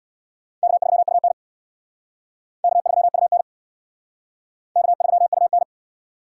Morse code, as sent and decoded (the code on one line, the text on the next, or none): -.- ...-- ..- -. / -.- ...-- ..- -. / -.- ...-- ..- -.
K3UN K3UN K3UN